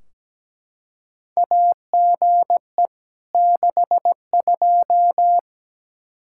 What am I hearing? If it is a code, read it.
Morse code, heard as a keyed tone, about 17 wpm, 700 Hz. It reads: AGE 62
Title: Practice QSO - Exchange 3 - More Info - 17wpm